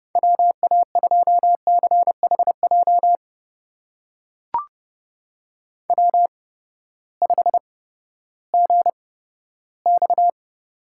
Morse code, heard as a keyed tone, about 30 wpm, spaced as for 15 wpm, 700 Hz. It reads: WA2CHJ W5GX
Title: USA Call Signs - 8x Character Spacing - 30wpm